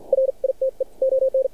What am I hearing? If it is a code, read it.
Morse code, heard as a keyed tone, about 59 wpm, spaced as for 46 wpm, 530 Hz. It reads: RITE ON
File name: archived / CRSnet